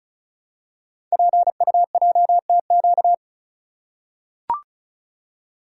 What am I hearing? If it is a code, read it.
Morse code, heard as a keyed tone, about 35 wpm, 690 Hz. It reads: PUJTQ
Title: Code Groups - 5 Letter:Numbers - 35wpm